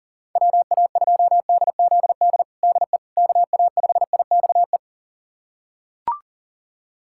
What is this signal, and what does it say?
Morse code, heard as a keyed tone, about 40 wpm, 700 Hz. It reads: WA2DZD DE KA5IXE
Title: USA Callsign DE Callsign - 6 Characters - 40wpm